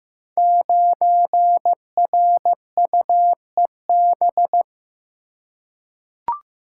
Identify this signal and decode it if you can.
Morse code, heard as a keyed tone, about 15 wpm, 700 Hz. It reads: 9RUEB
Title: Code Groups - 5 Letter:Numbers - 15wpm